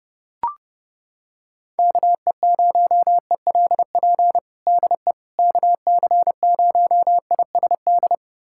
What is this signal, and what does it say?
Morse code, heard as a keyed tone, about 30 wpm, 700 Hz. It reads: KE0ELP DE KC0ISD
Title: USA Callsign DE Callsign - 6 Characters - 30wpm